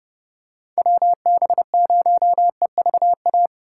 Morse code, heard as a keyed tone, about 29 wpm, 700 Hz. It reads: WB0EVA